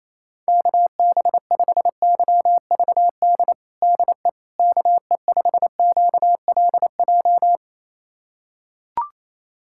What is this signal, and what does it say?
Morse code, heard as a keyed tone, about 28 wpm, 700 Hz. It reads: KB5YVD DE KE5QLJ
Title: USA Callsign DE Callsign - 6 Characters - 28wpm